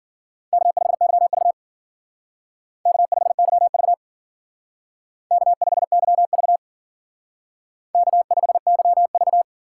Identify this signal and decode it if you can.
Morse code, heard as a keyed tone, about 54 wpm, 700 Hz. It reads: K5YV K5YV K5YV K5YV